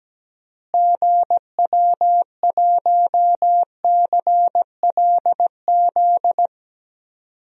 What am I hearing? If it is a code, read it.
Morse code, heard as a keyed tone, about 17 wpm, 700 Hz. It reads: GW1CLZ
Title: DXCC-100 Call Signs - 6 Characters - Repeated 3x - 17wpm